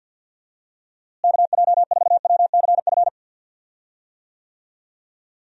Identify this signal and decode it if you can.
Morse code, heard as a keyed tone, about 47 wpm, 700 Hz. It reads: KJ4WCF
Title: USA Call Signs - 6 Characters - Repeated 3x - 50wpm